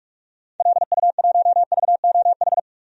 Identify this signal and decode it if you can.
Morse code, heard as a keyed tone, about 47 wpm, 700 Hz. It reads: LU1VOH
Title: DXCC-100 Call Signs - 6 Characters - 45wpm